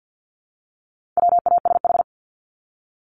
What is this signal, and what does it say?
Morse code, heard as a keyed tone, about 50 wpm, 700 Hz. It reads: WASH
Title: Top 1751-2000 Words 50wpm